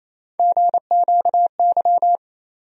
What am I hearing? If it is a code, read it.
Morse code, heard as a keyed tone, about 28 wpm, 700 Hz. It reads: GQY
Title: USA Call Sign Postfixes - 3 Letters 28wpm